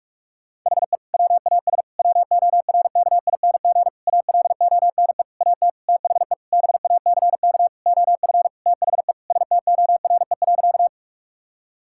Morse code, heard as a keyed tone, about 41 wpm, 700 Hz. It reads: HE WAS WORKING ALONE AT THE BACK OF THE STORE.
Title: Sentences from Top 1000 Words - Review Part 5 - 45wpm